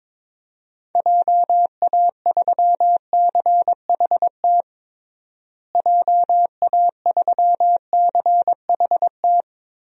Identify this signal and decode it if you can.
Morse code, heard as a keyed tone, about 22 wpm, 700 Hz. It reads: JA3CHT JA3CHT
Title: DXCC-100 Call Signs - 6 Characters - Repeated 3x - 22wpm